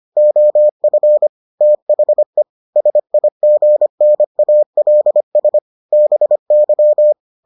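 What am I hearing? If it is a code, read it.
Morse code, heard as a keyed tone, about 25 wpm, 600 Hz. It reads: OF THE SIGNALS BY
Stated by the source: Morse edition of the Foundations of Amateur Radio episodes.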